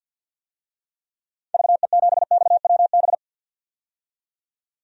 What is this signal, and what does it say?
Morse code, heard as a keyed tone, about 47 wpm, 700 Hz. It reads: VE7XWB
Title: DXCC-100 Call Signs - 6 Characters - 50wpm